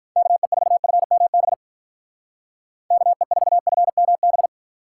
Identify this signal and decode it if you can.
Morse code, heard as a keyed tone, about 47 wpm, 700 Hz. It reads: KE4FMB KE4FMB